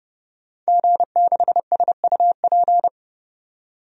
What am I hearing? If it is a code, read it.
Morse code, heard as a keyed tone, about 32 wpm, 700 Hz. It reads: G6SUP
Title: DXCC-100 Call Signs - 5 Characters - 30wpm